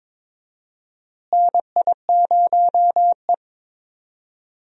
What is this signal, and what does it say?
Morse code, heard as a keyed tone, about 22 wpm, 700 Hz. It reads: NI0E
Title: USA Call Signs - 4 Characters - Repeated 2x - 22wpm